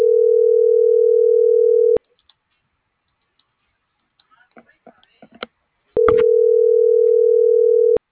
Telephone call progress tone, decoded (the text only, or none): ringback tone